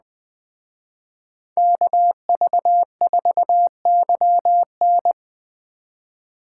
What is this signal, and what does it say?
Morse code, heard as a keyed tone, about 21 wpm, 700 Hz. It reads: KV4YN